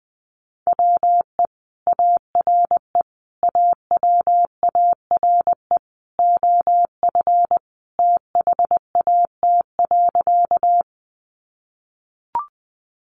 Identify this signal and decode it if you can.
Morse code, heard as a keyed tone, about 20 wpm, 700 Hz. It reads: WE ARE AWARE OF THAT.
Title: Sentences from Top 1200 Words 20wpm